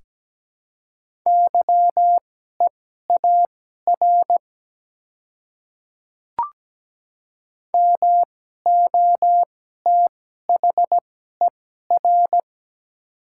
Morse code, heard as a keyed tone, about 17 wpm, spaced as for 12 wpm, 700 Hz. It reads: YEAR MOTHER
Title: 2x Character Spacing - Top 200 Words - 17wpm